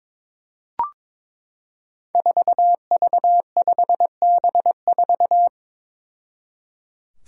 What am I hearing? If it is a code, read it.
Morse code, heard as a keyed tone, about 22 wpm, 700 Hz. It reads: 4V5B4
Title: Five Challenging Letter-Number Combinations 22wpm